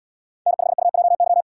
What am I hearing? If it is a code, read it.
Morse code, heard as a keyed tone, about 77 wpm, 690 Hz. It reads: N5UJY